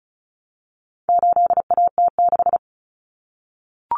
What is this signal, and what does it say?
Morse code, heard as a keyed tone, about 34 wpm, 700 Hz. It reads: 8AT6
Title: Common Cut Numbers - 4-Digit 35wpm